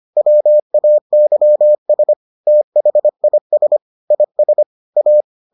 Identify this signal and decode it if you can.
Morse code, heard as a keyed tone, about 25 wpm, 600 Hz. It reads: WAYS THIS IS A